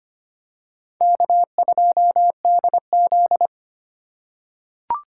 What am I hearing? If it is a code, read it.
Morse code, heard as a keyed tone, about 25 wpm, 700 Hz. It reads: K2DZ
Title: USA Call Signs - 4 Characters - 3x Character Spacing - 25wpm